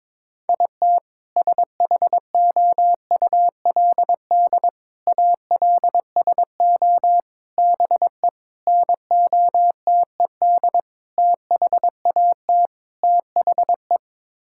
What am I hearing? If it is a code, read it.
Morse code, heard as a keyed tone, about 22 wpm, 700 Hz. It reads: IT SHOULD ALSO BE NOTED THAT THE
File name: practice / 22wpm / Sets of 7 Words - 22wpm